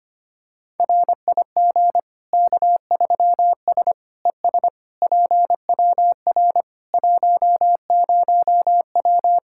Morse code, heard as a keyed tone, about 25 wpm, 700 Hz. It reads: RIG K3S ES PWR 10W
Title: Practice QSO - Exchange 2 - Info - 25wpm